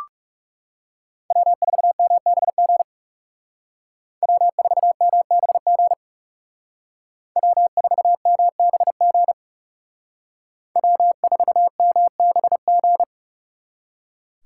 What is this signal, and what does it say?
Morse code, heard as a keyed tone, about 37 wpm, 700 Hz. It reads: W4MBG W4MBG W4MBG W4MBG
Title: Speed Racing - USA 1x3 Call Signs 30wpm